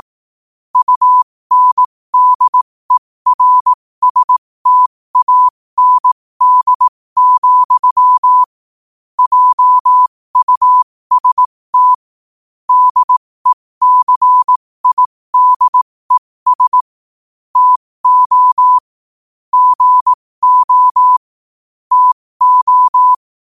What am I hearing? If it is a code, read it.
Morse code, heard as a keyed tone, about 18 wpm, spaced as for 15 wpm, 1000 Hz. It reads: UNDERSTAND, JUST DECIDES TO GO TO
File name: qotdfiles / 15wpm